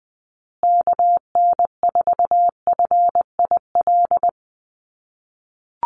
Morse code, heard as a keyed tone, about 20 wpm, 700 Hz. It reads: KN4FIL
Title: USA Call Signs Repeated 3x 20wpm